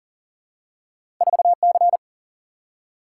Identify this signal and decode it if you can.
Morse code, heard as a keyed tone, about 40 wpm, 700 Hz. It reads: 4C
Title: Single-Letter Single-Number 40wpm